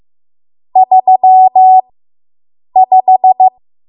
Morse code, heard as a keyed tone, about 15 wpm, spaced as for 7 wpm, 750 Hz. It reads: 35